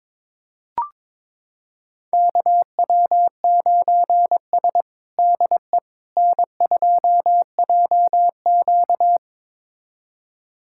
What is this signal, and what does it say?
Morse code, heard as a keyed tone, about 22 wpm, 700 Hz. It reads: KW9S DE N2JQ
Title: USA Callsign DE Callsign - 4 Characters - 22wpm